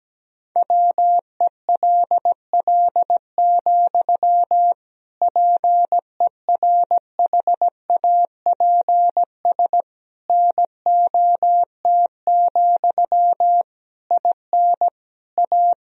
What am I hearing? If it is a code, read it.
Morse code, heard as a keyed tone, about 17 wpm, 700 Hz. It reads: WELL, PERHAPS NOT, IN A
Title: Sentences from Top 1000 Words - Review Part 2 - 17wpm